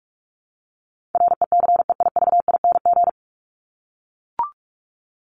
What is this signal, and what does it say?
Morse code, heard as a keyed tone, about 44 wpm, 700 Hz. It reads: RECEIVING
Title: ING Suffix - Top 100 Words with Base Words 45wpm